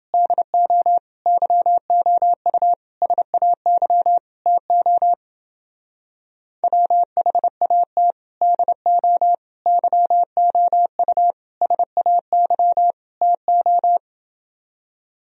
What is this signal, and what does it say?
Morse code, heard as a keyed tone, about 28 wpm, 700 Hz. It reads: DO YOU SAY TO WHAT DO YOU SAY TO